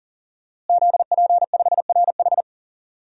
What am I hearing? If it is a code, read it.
Morse code, heard as a keyed tone, about 40 wpm, 700 Hz. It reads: ZP5RH
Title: DXCC-100 Call Signs - 5 Characters - Repeated 3x - 40wpm